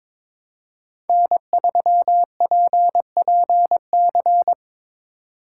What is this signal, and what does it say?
Morse code, heard as a keyed tone, about 22 wpm, 700 Hz. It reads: N3PPC
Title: USA Call Signs - 5 Characters - 2x Character Spacing - 22wpm